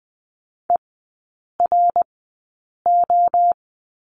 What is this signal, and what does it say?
Morse code, heard as a keyed tone, about 20 wpm, spaced as for 8 wpm, 700 Hz. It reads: ERO
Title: 053 Lesson 10e - Letter r - Three Characters 20wpm